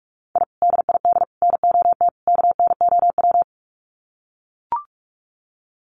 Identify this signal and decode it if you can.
Morse code, heard as a keyed tone, about 44 wpm, 700 Hz. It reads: I DID NOT KNOW
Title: Sets of 4 Words 45wpm